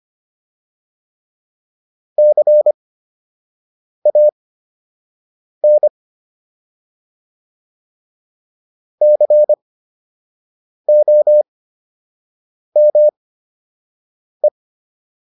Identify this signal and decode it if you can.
Morse code, heard as a keyed tone, about 25 wpm, spaced as for 6 wpm, 600 Hz. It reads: CAN COME